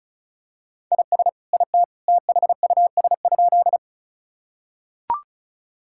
Morse code, heard as a keyed tone, about 36 wpm, 700 Hz. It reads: IS IT THUS?
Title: Sentences from Top 1000 Words - Review Part 5 - 35wpm